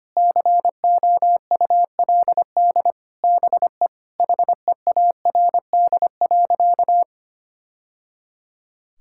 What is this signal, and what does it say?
Morse code, heard as a keyed tone, about 25 wpm, 700 Hz. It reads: COULD BE HEARD.
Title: Sentences from Top 1000 Words - Review Part 5 - 25wpm